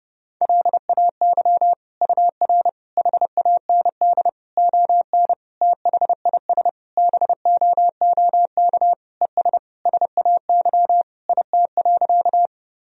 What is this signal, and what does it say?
Morse code, heard as a keyed tone, about 32 wpm, 700 Hz. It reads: LAY UR HAND ON THIS BOOK ES SAY IT.
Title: Sentences from Top 400 Words and Abbreviations 30wpm